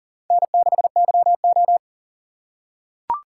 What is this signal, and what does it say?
Morse code, heard as a keyed tone, about 44 wpm, 700 Hz. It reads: N6YO
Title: USA Call Signs - 4 Characters - 3x Character Spacing - 40wpm